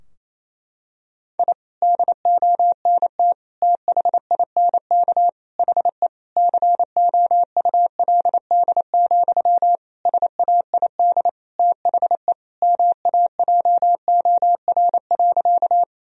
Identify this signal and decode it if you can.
Morse code, heard as a keyed tone, about 28 wpm, 700 Hz. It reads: I DONT THINK HE COULD, SAID THE MAJOR.
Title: Sentences from Top 1000 Words - Review Part 4 - 28wpm